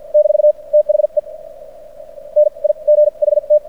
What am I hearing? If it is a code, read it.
Morse code, heard as a keyed tone, about 57 wpm, spaced as for 41 wpm, 600 Hz. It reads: - THE NIGHT
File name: archived / CRSnet